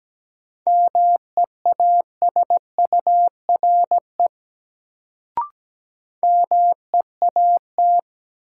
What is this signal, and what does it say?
Morse code, heard as a keyed tone, about 17 wpm, 700 Hz. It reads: MEASURE MEAT
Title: Sentences from Top 1200 Words - New Word Drill Alt - 17wpm